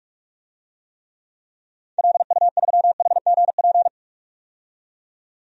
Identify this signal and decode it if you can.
Morse code, heard as a keyed tone, about 42 wpm, 700 Hz. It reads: LU3HGP
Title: DXCC-100 Call Signs - 6 Characters - 45wpm